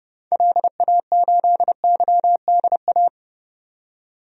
Morse code, heard as a keyed tone, about 30 wpm, 700 Hz. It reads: LA8YDA
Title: DXCC-100 Call Signs - 6 Characters - 30wpm